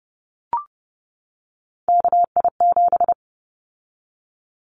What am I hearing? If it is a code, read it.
Morse code, heard as a keyed tone, about 30 wpm, 700 Hz. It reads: KI7
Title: Callsign Prefixes for Top 25 DXCC - 30wpm